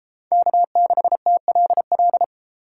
Morse code, heard as a keyed tone, about 31 wpm, 700 Hz. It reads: K6TLL